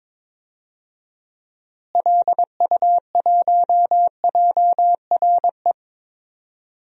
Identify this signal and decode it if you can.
Morse code, heard as a keyed tone, about 21 wpm, 700 Hz. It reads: LU1JRE
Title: DXCC-100 Call Signs - 6 Characters - Repeated 3x - 22wpm